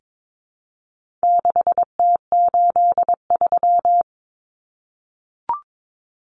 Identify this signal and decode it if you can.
Morse code, heard as a keyed tone, about 22 wpm, 700 Hz. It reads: 6T83
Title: Common Cut Numbers - 4-Digit 22wpm